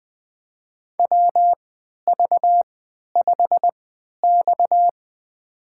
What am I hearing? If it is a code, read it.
Morse code, heard as a keyed tone, about 20 wpm, spaced as for 11 wpm, 700 Hz. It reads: WV5X
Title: USA Call Signs - 4 Characters - 3x Character Spacing - 20wpm